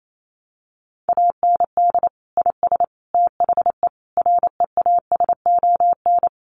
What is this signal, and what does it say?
Morse code, heard as a keyed tone, about 28 wpm, 700 Hz. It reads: AND IS THE REASON
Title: Sentences from Top 700 Words 28wpm